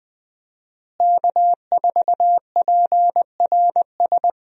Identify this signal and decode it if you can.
Morse code, heard as a keyed tone, about 20 wpm, 700 Hz. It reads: K4PRS
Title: USA Call Signs - 5 Characters - Repeated 3x - 20wpm